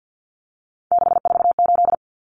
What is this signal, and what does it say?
Morse code, heard as a keyed tone, about 50 wpm, 700 Hz. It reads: 648
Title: Numbers - 3-Digit 50wpm